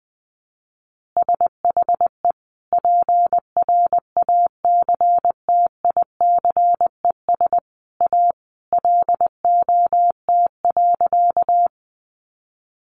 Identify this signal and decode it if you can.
Morse code, heard as a keyed tone, about 19 wpm, 700 Hz. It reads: SHE PRACTICES A LOT.
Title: Additional Easy Phrases 20wpm